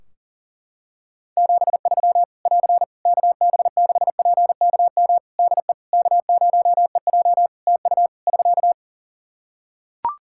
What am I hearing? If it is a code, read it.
Morse code, heard as a keyed tone, about 40 wpm, 700 Hz. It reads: 73 <AR> KB6PKM DE K0EJ TU <SK>
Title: Practice QSO - Exchange 4 - Ending - 40wpm